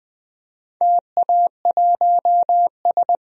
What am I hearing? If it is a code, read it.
Morse code, heard as a keyed tone, about 21 wpm, 700 Hz. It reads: TA1S
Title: DXCC-100 Call Signs - 4 Characters - 20wpm